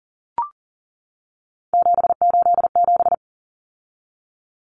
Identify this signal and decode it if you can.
Morse code, heard as a keyed tone, about 40 wpm, 700 Hz. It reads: 787
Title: Common Cut Numbers - 3-Digit 40wpm